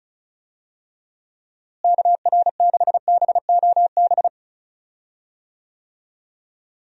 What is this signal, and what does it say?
Morse code, heard as a keyed tone, about 36 wpm, 700 Hz. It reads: KR6BOB
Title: USA Call Signs - 6 Characters - Repeated 3x - 35wpm